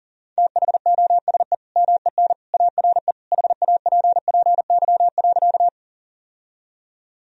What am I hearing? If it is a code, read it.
Morse code, heard as a keyed tone, about 37 wpm, 700 Hz. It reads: THOSE MEN ARE HAPPY.
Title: Sentences from Top 1000 Words - Review Part 1 - 40wpm